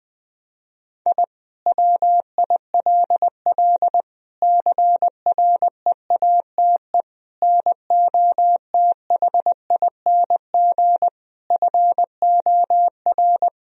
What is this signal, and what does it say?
Morse code, heard as a keyed tone, about 21 wpm, 700 Hz. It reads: I WILL CREATE NOTHING FOR
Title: Sentences from Top 1000 Words - Review Part 3 - 20wpm